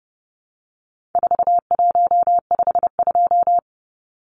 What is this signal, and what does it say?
Morse code, heard as a keyed tone, about 30 wpm, 700 Hz. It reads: 4152
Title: Numbers - 4-Digit 30wpm